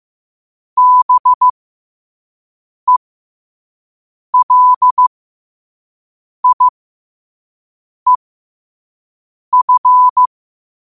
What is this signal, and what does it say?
Morse code, heard as a keyed tone, about 15 wpm, spaced as for 5 wpm, 1000 Hz. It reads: BELIEF